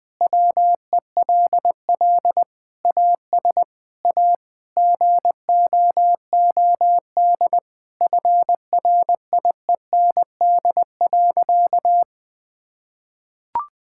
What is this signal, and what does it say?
Morse code, heard as a keyed tone, about 21 wpm, 700 Hz. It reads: WELL AS A GOOD FRIEND.
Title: Sentences with 4-Word Phrases - Spoken Beginning - 20wpm